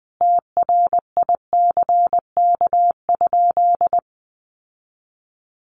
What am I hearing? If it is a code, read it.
Morse code, heard as a keyed tone, about 20 wpm, 700 Hz. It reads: TRICK?
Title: Sentences from Top 1500 Words 20wpm